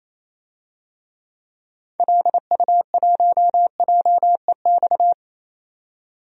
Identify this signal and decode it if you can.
Morse code, heard as a keyed tone, about 28 wpm, 700 Hz. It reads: LU1JEX
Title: DXCC-100 Call Signs - 6 Characters - 28wpm